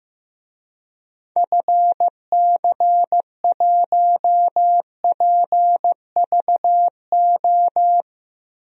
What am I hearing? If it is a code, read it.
Morse code, heard as a keyed tone, about 15 wpm, 700 Hz. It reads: FC1PVO
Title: DXCC-100 Call Signs - 6 Characters - 15wpm